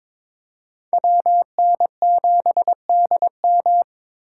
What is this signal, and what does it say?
Morse code, heard as a keyed tone, about 26 wpm, 700 Hz. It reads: WN7DM
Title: Speed Racing - USA 2x2 Call Signs 22wpm